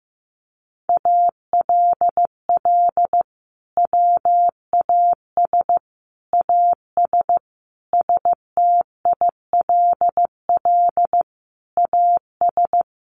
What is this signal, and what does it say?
Morse code, heard as a keyed tone, about 15 wpm, 700 Hz. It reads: ALL WAS AS STILL AS
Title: Sentences from Top 800 Words 15wpm